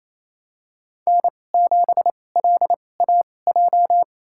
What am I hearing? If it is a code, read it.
Morse code, heard as a keyed tone, about 28 wpm, spaced as for 20 wpm, 700 Hz. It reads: N7LAJ
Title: USA Call Signs - 5 Characters - 2x Character Spacing - 28wpm